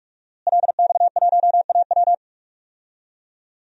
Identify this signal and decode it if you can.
Morse code, heard as a keyed tone, about 46 wpm, 700 Hz. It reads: LX1AW
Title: DXCC-100 Call Signs - 5 Characters - Repeated 3x - 45wpm